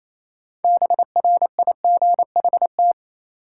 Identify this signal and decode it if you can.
Morse code, heard as a keyed tone, about 28 wpm, 700 Hz. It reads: BRIGHT
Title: Sentences from Top 1000 Words - New Word Drill - 28wpm